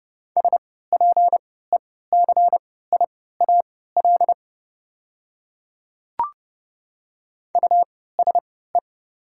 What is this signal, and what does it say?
Morse code, heard as a keyed tone, about 30 wpm, spaced as for 17 wpm, 690 Hz. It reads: SPECIAL USE
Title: 3x Character Spacing - Top 500 Words - 30wpm